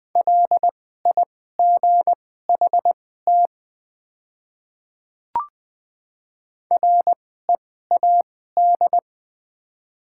Morse code, heard as a keyed tone, about 20 wpm, spaced as for 14 wpm, 700 Hz. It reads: LIGHT READ
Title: 2x Character Spacing - Top 300 Words - 20wpm